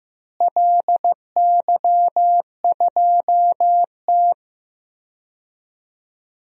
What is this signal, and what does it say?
Morse code, heard as a keyed tone, about 15 wpm, 700 Hz. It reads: LY2T
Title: DXCC-100 Call Signs - 4 Characters - Repeated 2x - 15wpm